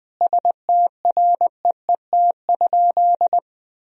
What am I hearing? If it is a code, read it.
Morse code, heard as a keyed tone, about 20 wpm, 700 Hz. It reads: STREET?
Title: Sentences from Top 1000 Words - Review Part 3 - 20wpm